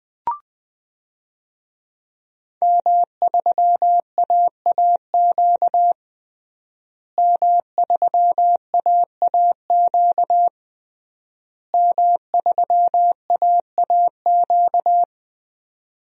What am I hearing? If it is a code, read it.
Morse code, heard as a keyed tone, about 20 wpm, 700 Hz. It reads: M3AAQ M3AAQ M3AAQ
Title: DXCC-100 Call Signs - 5 Characters - Repeated 3x - 20wpm